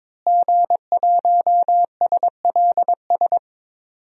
Morse code, heard as a keyed tone, about 22 wpm, 700 Hz. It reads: G1SLS
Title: DXCC-100 Call Signs - 5 Characters - Repeated 2x - 22wpm